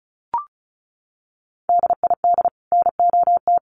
Morse code, heard as a keyed tone, about 34 wpm, 700 Hz. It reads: DID NOT
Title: Sets of 5 Words Encore2 35wpm